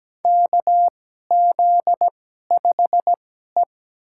Morse code, heard as a keyed tone, about 17 wpm, spaced as for 12 wpm, 700 Hz. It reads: KZ5E
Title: USA Call Signs - 4 Characters - 2x Character Spacing - 17wpm